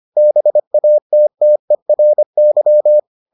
Morse code, heard as a keyed tone, about 25 wpm, 600 Hz. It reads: BATTERY